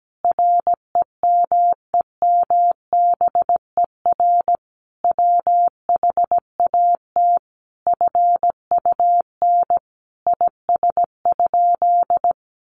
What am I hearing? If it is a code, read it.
Morse code, heard as a keyed tone, about 17 wpm, 700 Hz. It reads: REMEMBER WHAT FUN IS?
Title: Sentences from Top 700 Words 17wpm